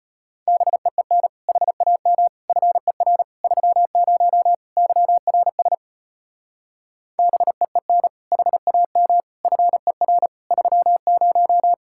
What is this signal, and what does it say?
Morse code, heard as a keyed tone, about 33 wpm, 700 Hz. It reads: BEEN HAM FER 30 YRS BEEN HAM FER 30